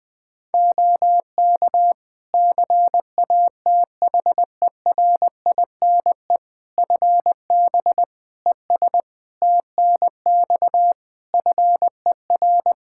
Morse code, heard as a keyed tone, about 20 wpm, 700 Hz. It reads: OK CATHERINE FB ES TNX FER
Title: Practice QSO - Exchange 2 - Info - 20wpm